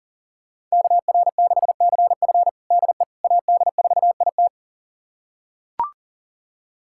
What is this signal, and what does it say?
Morse code, heard as a keyed tone, about 40 wpm, 700 Hz. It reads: KR6CF DE AD4IT
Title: USA Callsign DE Callsign - 5 Characters - 40wpm